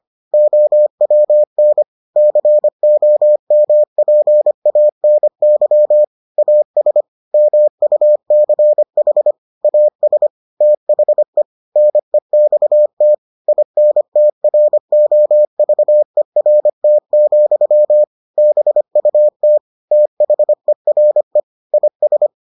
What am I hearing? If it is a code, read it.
Morse code, heard as a keyed tone, about 25 wpm, 600 Hz. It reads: OWN COMPANY AS MUCH AS THE NEXT INTROVERT, BUT THERE IS